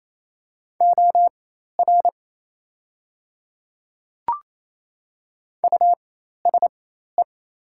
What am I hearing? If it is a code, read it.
Morse code, heard as a keyed tone, about 28 wpm, spaced as for 13 wpm, 690 Hz. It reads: OR USE
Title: 4x Character Spacing - Top 200 Words - 28wpm